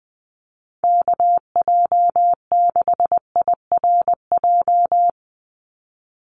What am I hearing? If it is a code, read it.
Morse code, heard as a keyed tone, about 20 wpm, 700 Hz. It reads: KJ6IRJ
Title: USA Call Signs 20wpm